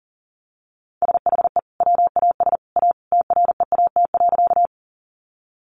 Morse code, heard as a keyed tone, about 40 wpm, 700 Hz. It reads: SHE WAS A TREAT.
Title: Sentences from Top 1300 Words 40wpm